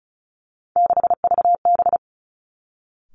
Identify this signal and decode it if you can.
Morse code, heard as a keyed tone, about 36 wpm, 700 Hz. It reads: 6VB
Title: Three Challenging Letter-Number Combinations 35wpm